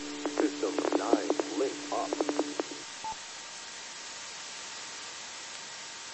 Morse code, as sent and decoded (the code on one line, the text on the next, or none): none